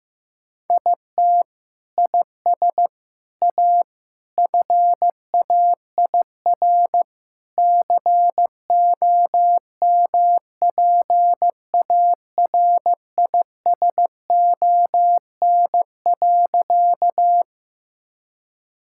Morse code, heard as a keyed tone, about 16 wpm, 700 Hz. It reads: IT IS A FAIR COMPARISON.
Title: Sentences from Top 1600 Words 15wpm